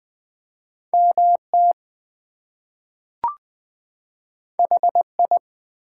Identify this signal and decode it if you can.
Morse code, heard as a keyed tone, about 20 wpm, 700 Hz. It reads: MT HI
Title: US States and Canadian Provinces 20wpm